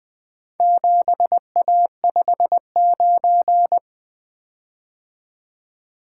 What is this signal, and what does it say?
Morse code, heard as a keyed tone, about 20 wpm, 700 Hz. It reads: 7A59
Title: Four Letter-Number Combinations - 20wpm